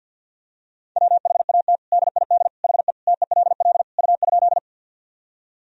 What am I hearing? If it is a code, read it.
Morse code, heard as a keyed tone, about 51 wpm, 700 Hz. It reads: WHAT DID HE TELL U?
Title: Sentences from Top 200 Words and Abbreviations 50wpm